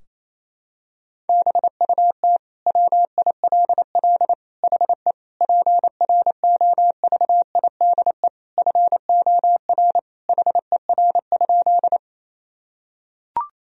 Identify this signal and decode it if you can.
Morse code, heard as a keyed tone, about 28 wpm, 700 Hz. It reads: BUT WILL HE PROVIDE FOR HER?
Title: Sentences from Top 1000 Words - Review Part 3 - 28wpm